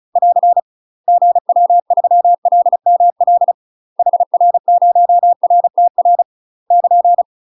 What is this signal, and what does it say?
Morse code, heard as a keyed tone, about 35 wpm, 700 Hz. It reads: <AR> GW3LML HR0RTR <KN>
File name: UK MASTER QSOs TRAINING / 35WPM / UK MASTER QSOs 0029